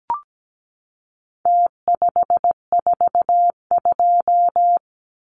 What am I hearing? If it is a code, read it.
Morse code, heard as a keyed tone, about 17 wpm, 700 Hz. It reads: T542
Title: Numbers - Contest Style - 4-Digit 17wpm